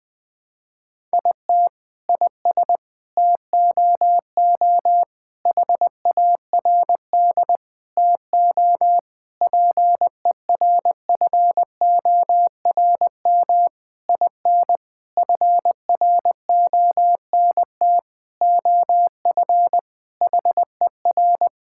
Morse code, heard as a keyed tone, about 20 wpm, 700 Hz. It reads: IT IS TOO HARD TO PERFORM IN FRONT OF HER
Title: Sentences from Top 1000 Words - Review Part 5 - 20wpm